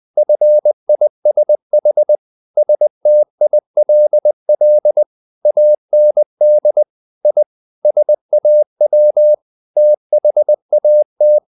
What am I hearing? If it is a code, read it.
Morse code, heard as a keyed tone, about 20 wpm, 600 Hz. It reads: FISH STILL AND I SAW THAT
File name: the old man and the sea part 2 20wpm